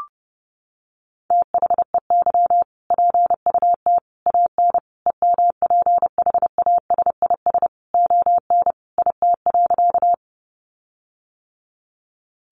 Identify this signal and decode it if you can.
Morse code, heard as a keyed tone, about 30 wpm, 700 Hz. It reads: THEY PUT AN EMPHASIS ON IT.
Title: Sentences from Top 1400 Words 30wpm